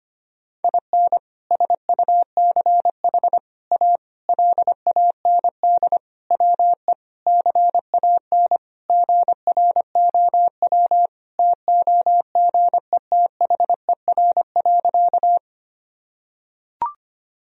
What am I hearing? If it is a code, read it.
Morse code, heard as a keyed tone, about 25 wpm, 700 Hz. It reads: IN SUCH A LAND WE CAN GROW TOGETHER.
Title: Sentences from Top 1000 Words - Review Part 1 - 25wpm